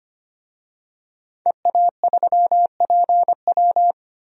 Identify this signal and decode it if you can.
Morse code, heard as a keyed tone, about 25 wpm, 700 Hz. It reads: EA3PW
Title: DXCC-100 Call Signs - 5 Characters - 25wpm